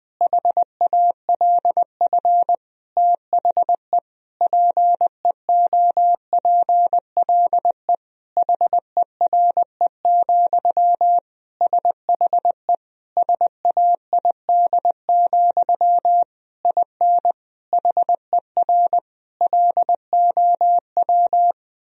Morse code, heard as a keyed tone, about 20 wpm, 700 Hz. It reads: HALF THE PEOPLE HERE, SHE SAID, IN HER LOW
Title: Sentences from Top 1000 Words - Review Part 1 - 20wpm